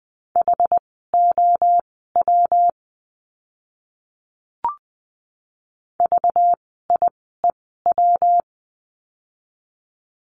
Morse code, heard as a keyed tone, about 20 wpm, spaced as for 14 wpm, 700 Hz. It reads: HOW VIEW
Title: 192 Lesson 31f - Letter v - Words 20wpm